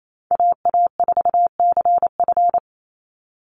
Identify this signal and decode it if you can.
Morse code, heard as a keyed tone, about 28 wpm, 700 Hz. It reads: AA4CF
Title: Call Signs - 28wpm